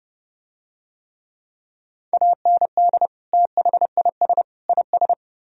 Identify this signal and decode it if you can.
Morse code, heard as a keyed tone, about 30 wpm, 700 Hz. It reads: AND THIS IS